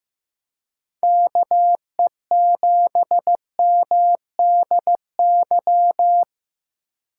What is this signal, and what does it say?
Morse code, heard as a keyed tone, about 18 wpm, 700 Hz. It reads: KE7MDY
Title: Speed Racing - USA 2x3 Call Signs 15wpm